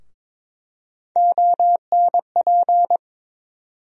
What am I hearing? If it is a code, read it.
Morse code, heard as a keyed tone, about 22 wpm, 700 Hz. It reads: ONP
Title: USA Call Sign Postfixes - 3 Letters 22wpm